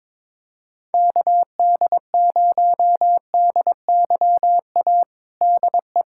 Morse code, heard as a keyed tone, about 22 wpm, 700 Hz. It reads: KD0DYA DE
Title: USA Callsign DE Callsign 22wpm